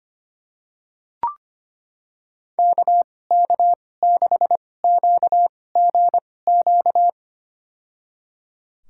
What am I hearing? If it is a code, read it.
Morse code, heard as a keyed tone, about 25 wpm, spaced as for 18 wpm, 700 Hz. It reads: KK6QGQ
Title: USA Call Signs - 6 Characters - 2x Character Spacing - 25wpm